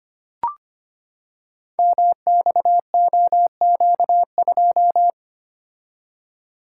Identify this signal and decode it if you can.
Morse code, heard as a keyed tone, about 25 wpm, 700 Hz. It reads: MXOQ2
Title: Code Groups - 5 Letter:Numbers - 25wpm